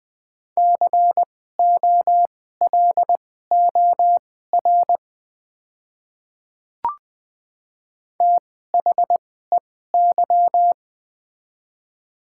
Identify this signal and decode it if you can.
Morse code, heard as a keyed tone, about 20 wpm, spaced as for 15 wpm, 700 Hz. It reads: COLOR THEY
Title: 2x Character Spacing - Top 300 Words - 20wpm